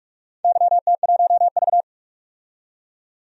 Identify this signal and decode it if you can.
Morse code, heard as a keyed tone, about 46 wpm, 700 Hz. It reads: YT1V
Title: DXCC-100 Call Signs - 4 Characters - 45wpm